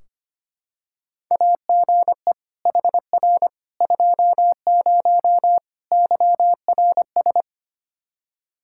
Morse code, heard as a keyed tone, about 25 wpm, 700 Hz. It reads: AGE HR 20 YRS
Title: Practice QSO - Exchange 3 - More Info - 25wpm